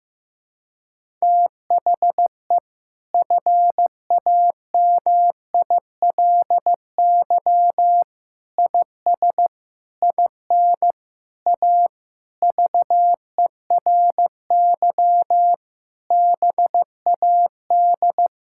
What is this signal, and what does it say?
Morse code, heard as a keyed tone, about 16 wpm, 700 Hz. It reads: THE FAMILY IS IN A VERY BAD
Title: Sentences from Top 1000 Words - Review Part 5 - 15wpm